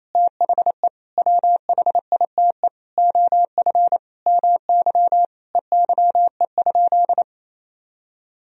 Morse code, heard as a keyed tone, about 28 wpm, 700 Hz. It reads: THE WHITE OF MY EYE?
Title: Sentences from Top 500 Words and Abbreviations 28wpm